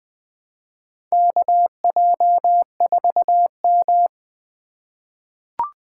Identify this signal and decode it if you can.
Morse code, heard as a keyed tone, about 20 wpm, 700 Hz. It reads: KJ4M
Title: USA Call Signs - 4 Characters - 2x Character Spacing - 20wpm